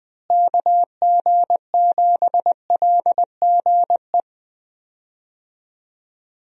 Morse code, heard as a keyed tone, about 20 wpm, 700 Hz. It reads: KG7LGE
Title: USA Call Signs - 6 Characters - Repeated 3x - 20wpm